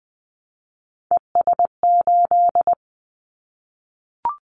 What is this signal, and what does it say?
Morse code, heard as a keyed tone, about 20 wpm, 700 Hz. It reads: ES8
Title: Callsign Prefixes for Top 100 DXCC - 20wpm